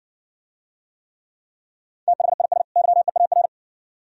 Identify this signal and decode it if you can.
Morse code, heard as a keyed tone, about 60 wpm, 690 Hz. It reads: THIS YEAR